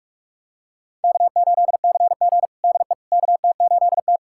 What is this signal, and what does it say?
Morse code, heard as a keyed tone, about 46 wpm, 700 Hz. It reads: K8CG DE KT8T
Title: USA Callsign DE Callsign - 4 Characters - 45wpm